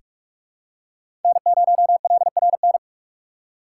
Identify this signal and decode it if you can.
Morse code, heard as a keyed tone, about 42 wpm, 700 Hz. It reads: N0LRN
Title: USA Call Signs - 5 Characters - 4x Character Spacing - 45wpm